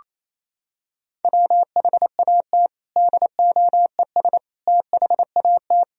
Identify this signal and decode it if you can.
Morse code, heard as a keyed tone, about 28 wpm, 700 Hz. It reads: WHAT DOES THAT
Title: Sentences from Top 1000 Words - Review Part 1 - 28wpm